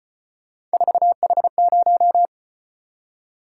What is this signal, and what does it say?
Morse code, heard as a keyed tone, about 33 wpm, 700 Hz. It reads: 4H0